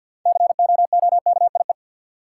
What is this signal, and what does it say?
Morse code, heard as a keyed tone, about 52 wpm, 700 Hz. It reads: COOKIE
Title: Six Letter Words - 50wpm